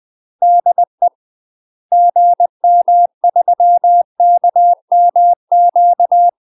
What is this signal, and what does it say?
Morse code, heard as a keyed tone, about 20 wpm, 700 Hz. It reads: DE GM3KMQ
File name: UK MASTER QSOs TRAINING / 20WPM / UK MASTER QSOs 0052